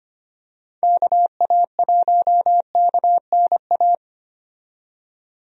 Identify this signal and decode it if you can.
Morse code, heard as a keyed tone, about 25 wpm, 700 Hz. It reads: KA1KNA
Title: USA Call Signs - 6 Characters - 25wpm